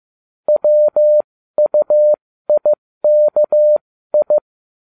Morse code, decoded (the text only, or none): WUIKI